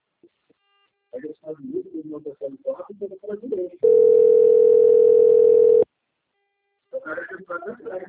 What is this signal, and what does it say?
Ringback tone (North America), 440 + 480 Hz, with speech.
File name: rg-1600-unknown- - - .7360